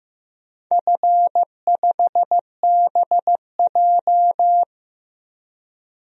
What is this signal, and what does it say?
Morse code, heard as a keyed tone, about 15 wpm, 700 Hz. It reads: F5BJ
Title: DXCC-100 Call Signs - 4 Characters - 15wpm